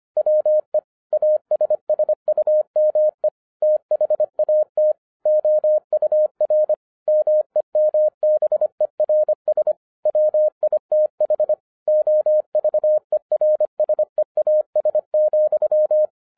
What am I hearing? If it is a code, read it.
Morse code, heard as a keyed tone, about 25 wpm, 610 Hz. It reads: WE ASSUME THAT OUR MEMBERS WITH OVERSEAS,
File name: 2012-01-mp3 / n14-FUN WITH NUMBERS-25wpm